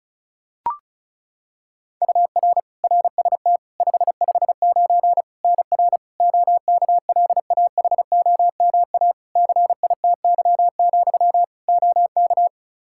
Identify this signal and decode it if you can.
Morse code, heard as a keyed tone, about 34 wpm, 700 Hz. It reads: UR RST 559 NR OKLAHOMA CITY, OK
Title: Practice QSO - Exchange 1 - Beginning - 35wpm